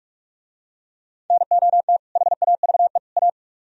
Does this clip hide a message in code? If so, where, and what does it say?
Yes: Morse code from 1.3 to 3.3 s, heard as a keyed tone, about 48 wpm, 700 Hz. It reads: NOT HAVE A